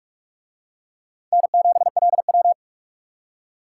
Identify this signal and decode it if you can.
Morse code, heard as a keyed tone, about 38 wpm, 700 Hz. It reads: N7LW